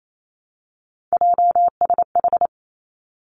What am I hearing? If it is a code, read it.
Morse code, heard as a keyed tone, about 28 wpm, 700 Hz. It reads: JSH
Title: Three Challenging Letter-Number Combinations 28wpm